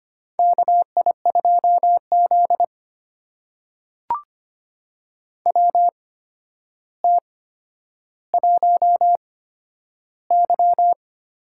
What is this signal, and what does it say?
Morse code, heard as a keyed tone, about 25 wpm, spaced as for 11 wpm, 700 Hz. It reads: KI2Z WT1Y
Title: USA Call Signs - 4 Characters - 8x Character Spacing - 25wpm